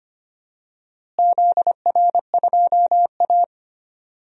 Morse code, heard as a keyed tone, about 25 wpm, 700 Hz. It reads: ZR2A
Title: DXCC-100 Call Signs - 4 Characters - Repeated 2x - 25wpm